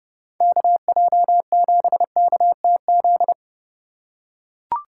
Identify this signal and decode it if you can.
Morse code, heard as a keyed tone, about 30 wpm, 700 Hz. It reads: KJ7KTZ
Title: USA Call Signs - 6 Characters - 4x Character Spacing - 30wpm